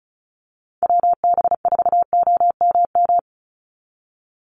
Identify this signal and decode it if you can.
Morse code, heard as a keyed tone, about 34 wpm, 700 Hz. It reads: WB4OMM
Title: Call Signs Repeated 2x - 35wpm